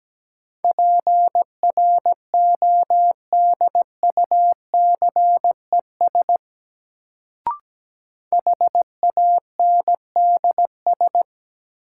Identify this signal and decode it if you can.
Morse code, heard as a keyed tone, about 17 wpm, 700 Hz. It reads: PRODUCES HANDS
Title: Sentences from Top 1000 Words - Review Part 6 - New Word Drill - 17wpm